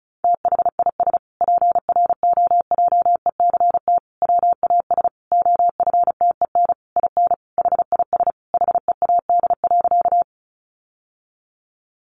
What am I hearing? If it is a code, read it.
Morse code, heard as a keyed tone, about 34 wpm, 700 Hz. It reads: THIS PROJECT WAS OFTEN IN HIS HEAD.
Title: Sentences from Top 1400 Words 35wpm